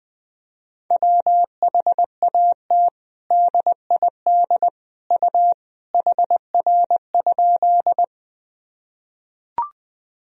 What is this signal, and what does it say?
Morse code, heard as a keyed tone, about 20 wpm, 700 Hz. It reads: WHAT DID U HR?
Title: Sentences from Top 300 Words and Abbreviations 20wpm